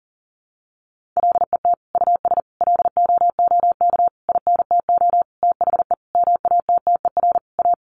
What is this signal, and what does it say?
Morse code, heard as a keyed tone, about 40 wpm, 700 Hz. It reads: LET US LOOK INTO THE MATTER A
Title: Sentences from Top 1400 Words 40wpm